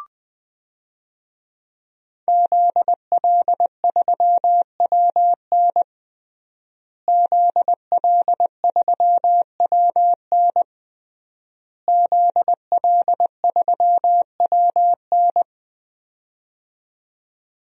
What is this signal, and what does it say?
Morse code, heard as a keyed tone, about 20 wpm, 700 Hz. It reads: ZL3WN ZL3WN ZL3WN
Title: DXCC-100 Call Signs - 5 Characters - Repeated 3x - 20wpm